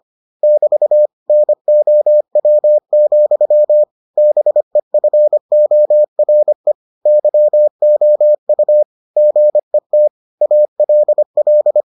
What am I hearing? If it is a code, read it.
Morse code, heard as a keyed tone, about 25 wpm, 600 Hz. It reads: <BT> NOW, BEFORE YOU GET ALL